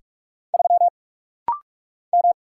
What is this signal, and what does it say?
Morse code, heard as a keyed tone, about 44 wpm, 700 Hz. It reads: 3 M
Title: Single Letter-Number Alt - Rapid-Fire 45wpm